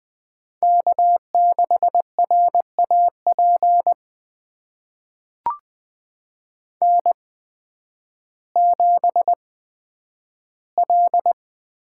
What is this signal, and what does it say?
Morse code, heard as a keyed tone, about 20 wpm, spaced as for 11 wpm, 700 Hz. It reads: K6RAP N7L
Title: USA Call Signs - 5 Characters - 8x Character Spacing - 20wpm